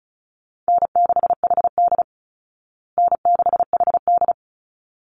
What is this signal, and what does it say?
Morse code, heard as a keyed tone, about 34 wpm, 700 Hz. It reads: N6HD N6HD
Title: Easy Call Signs Repeated 2x - 35wpm